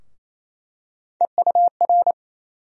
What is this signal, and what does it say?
Morse code, heard as a keyed tone, about 28 wpm, 700 Hz. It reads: EUR
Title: USA Call Sign Postfixes - 3 Letters 28wpm